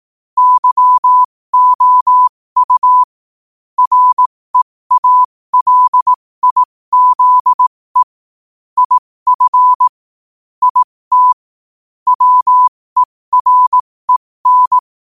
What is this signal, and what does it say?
Morse code, heard as a keyed tone, about 18 wpm, spaced as for 15 wpm, 1000 Hz. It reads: YOU REALIZE IF IT WEREN
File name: qotdfiles / 15wpm